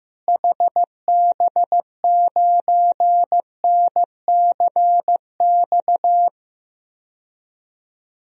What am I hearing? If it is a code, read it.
Morse code, heard as a keyed tone, about 15 wpm, 700 Hz. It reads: HB9NCX
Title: DXCC-100 Call Signs - 6 Characters - Repeated 2x - 15wpm